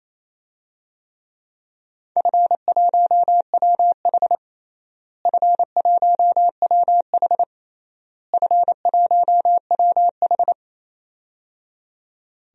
Morse code, heard as a keyed tone, about 28 wpm, 700 Hz. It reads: F1WH F1WH F1WH
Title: DXCC-100 Call Signs - 4 Characters - Repeated 3x - 28wpm